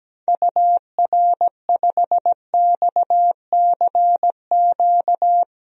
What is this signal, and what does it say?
Morse code, heard as a keyed tone, about 17 wpm, 700 Hz. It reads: UR5XCQ
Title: DXCC-100 Call Signs - 6 Characters - 17wpm